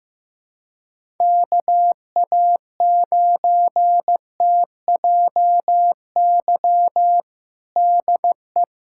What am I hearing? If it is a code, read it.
Morse code, heard as a keyed tone, about 15 wpm, 700 Hz. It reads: KA9TJY DE
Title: USA Callsign DE Callsign 15wpm